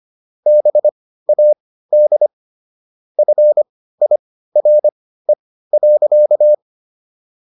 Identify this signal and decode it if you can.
Morse code, heard as a keyed tone, about 25 wpm, spaced as for 15 wpm, 600 Hz. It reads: BAD FIRE.